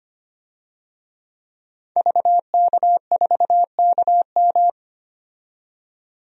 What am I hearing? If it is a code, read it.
Morse code, heard as a keyed tone, about 25 wpm, 700 Hz. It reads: VK4KM
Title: DXCC-100 Call Signs - 5 Characters - 25wpm